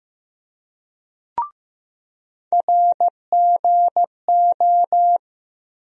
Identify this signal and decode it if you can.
Morse code, heard as a keyed tone, about 16 wpm, 700 Hz. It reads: RGO
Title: USA Call Sign Postfixes - 3 Letters 15wpm